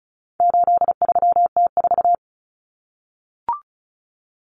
Morse code, heard as a keyed tone, about 34 wpm, 700 Hz. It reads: 83T4
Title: Common Cut Numbers - 4-Digit 35wpm